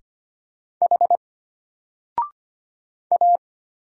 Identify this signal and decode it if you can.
Morse code, heard as a keyed tone, about 25 wpm, 700 Hz. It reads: H A
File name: Single-Letter-Number-25wpm